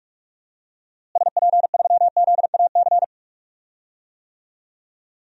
Morse code, heard as a keyed tone, about 46 wpm, 700 Hz. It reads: SP3ZAC
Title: DXCC-100 Call Signs - 6 Characters - Repeated 3x - 45wpm